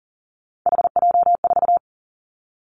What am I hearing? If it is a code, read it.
Morse code, heard as a keyed tone, about 41 wpm, 700 Hz. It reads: HJ4
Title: Callsign Prefixes for Top 100 DXCC - 40wpm